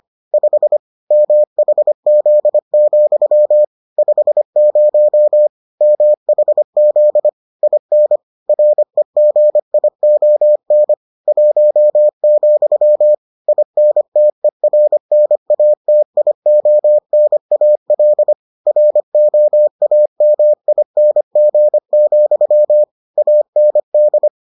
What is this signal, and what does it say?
Morse code, heard as a keyed tone, about 25 wpm, 600 Hz. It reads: 5 MHZ, 50 MHZ IN REGION 1, INTERNATIONAL ROAMING, AND